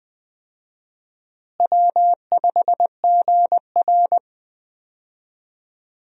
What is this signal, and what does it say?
Morse code, heard as a keyed tone, about 21 wpm, 700 Hz. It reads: W5GR
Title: USA Call Signs - 4 Characters - Repeated 3x - 20wpm